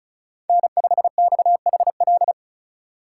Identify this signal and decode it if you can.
Morse code, heard as a keyed tone, about 36 wpm, 700 Hz. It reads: N5XHL
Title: USA Call Signs - 4x Character Spacing - 35wpm